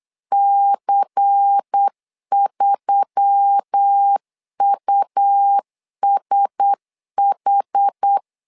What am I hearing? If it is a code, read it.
Morse code, heard as a keyed tone, about 8 wpm, 800 Hz. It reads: C3USH
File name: LEZIONI / 5 LBVC / 5 LBVC 8wpm 2935